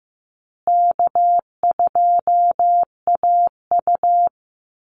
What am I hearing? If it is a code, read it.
Morse code, heard as a keyed tone, about 15 wpm, 700 Hz. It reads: K2AU
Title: Call Signs Repeated 3x - 15wpm